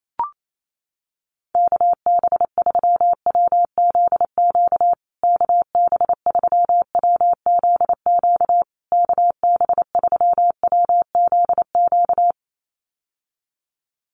Morse code, heard as a keyed tone, about 28 wpm, 700 Hz. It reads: KB3WZQ KB3WZQ KB3WZQ
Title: USA Call Signs Repeated 3x 28wpm